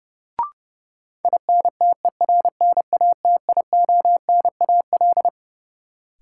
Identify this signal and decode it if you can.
Morse code, heard as a keyed tone, about 30 wpm, 700 Hz. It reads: INTERNATIONAL
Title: Sentences from Top 1000 Words - Review Part 4 - New Word Drill - 30wpm